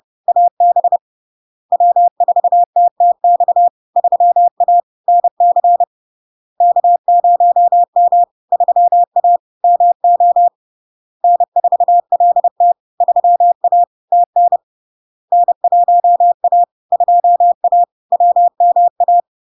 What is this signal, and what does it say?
Morse code, heard as a keyed tone, about 30 wpm, 700 Hz. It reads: AB W4TTX 3A NC K0M 3A MO N4LT 3A TN N1A 2A WMA
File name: LARK-FD-2022-Log-in-CW-30wpm-700-Hz